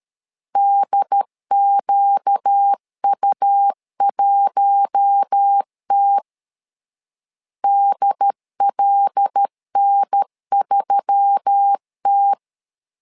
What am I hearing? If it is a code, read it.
Morse code, heard as a keyed tone, about 13 wpm, 800 Hz. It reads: DQU1T DLN3T
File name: LEZIONI / 5 LBVC / 5 L 12wpm 1913